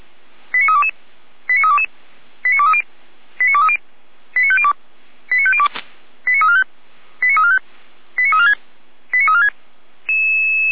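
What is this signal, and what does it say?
Five-tone selective call (ZVEI), heard as quick runs of five tones, tones of about 70 ms. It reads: 89322 89322 89230 89230 89502 89502 89345 89345 89346 89346
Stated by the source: ZVEI five-tone call 89305 received by SDR, archived by feuerwehr-orlamuende (Orlamünde fire brigade).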